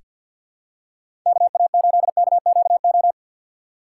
Morse code, heard as a keyed tone, about 47 wpm, 700 Hz. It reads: KA8KQO
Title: USA Call Signs - 6 Characters - 6x Character Spacing - 50wpm